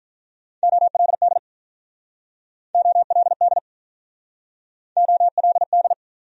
Morse code, heard as a keyed tone, about 45 wpm, 700 Hz. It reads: OLD OLD OLD